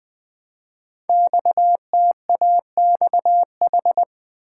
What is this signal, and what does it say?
Morse code, heard as a keyed tone, about 20 wpm, 700 Hz. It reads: XTAXH
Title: Code Groups - 5 Letters - 20wpm